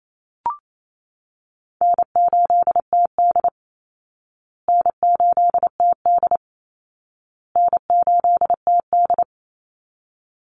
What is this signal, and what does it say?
Morse code, heard as a keyed tone, about 28 wpm, 700 Hz. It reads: N8TD N8TD N8TD
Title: Easy Call Signs Repeated 3x - 28wpm